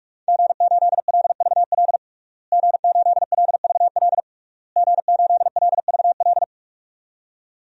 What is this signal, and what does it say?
Morse code, heard as a keyed tone, about 44 wpm, 700 Hz. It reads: G8LVL G8LVL G8LVL
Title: DXCC-100 Call Signs - 5 Characters - Repeated 3x - 45wpm